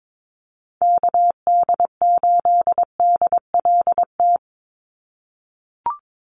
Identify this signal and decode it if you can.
Morse code, heard as a keyed tone, about 22 wpm, 700 Hz. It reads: KD8DLT
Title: USA Call Signs 22wpm